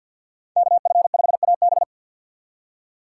Morse code, heard as a keyed tone, about 51 wpm, 700 Hz. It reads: KF5AB
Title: USA Call Signs - 5 Characters - 3x Character Spacing - 50wpm